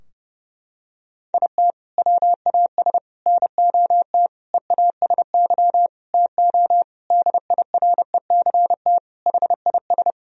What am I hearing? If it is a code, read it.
Morse code, heard as a keyed tone, about 30 wpm, 700 Hz. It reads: IT WAS NOT EASY TO DIRECT HIS
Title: Sentences from Top 1000 Words - Review Part 2 - 30wpm